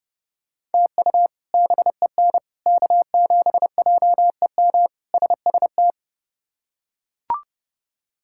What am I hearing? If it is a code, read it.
Morse code, heard as a keyed tone, about 30 wpm, 700 Hz. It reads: TU BEN K7JEM SST
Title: K1USN SST QSO Practice - 30wpm